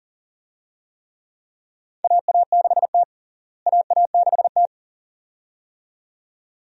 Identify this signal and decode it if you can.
Morse code, heard as a keyed tone, about 40 wpm, 700 Hz. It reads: AA6T AA6T
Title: USA Call Signs - 4 Characters - Repeated 2x - 40wpm